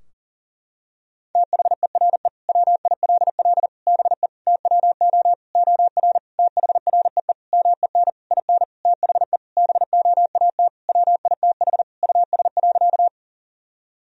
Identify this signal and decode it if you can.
Morse code, heard as a keyed tone, about 37 wpm, 700 Hz. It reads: THERE WILL BE TWO OR THREE MEN IN THE BOAT WITH US.
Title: Sentences from Top 1000 Words - Review Part 3 - 40wpm